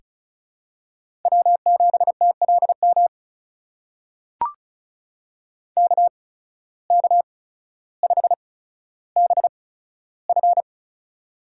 Morse code, heard as a keyed tone, about 36 wpm, spaced as for 15 wpm, 700 Hz. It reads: W7TLM KK5BF
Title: USA Call Signs - 5 Characters - 8x Character Spacing - 35wpm